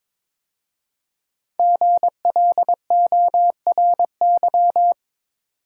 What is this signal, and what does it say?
Morse code, heard as a keyed tone, about 23 wpm, 700 Hz. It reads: GLORY
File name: 5 Letter Words - First 3 Letters Given - 22wpm